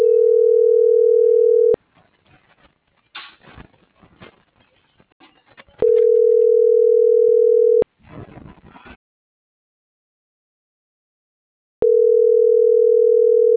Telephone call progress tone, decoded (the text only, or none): ringback tone